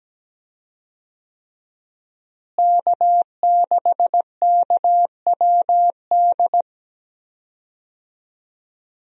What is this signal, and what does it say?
Morse code, heard as a keyed tone, about 17 wpm, 700 Hz. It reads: K6KWD
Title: USA Call Signs - 5 Characters - Repeated 3x - 17wpm